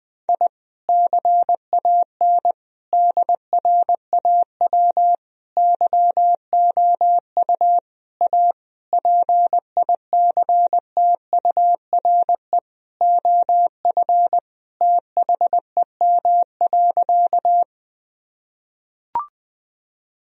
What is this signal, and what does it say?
Morse code, heard as a keyed tone, about 20 wpm, 700 Hz. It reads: I CAN DRAW YOU A PICTURE OF THEM.
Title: Sentences from Top 1000 Words - Review Part 1 - 20wpm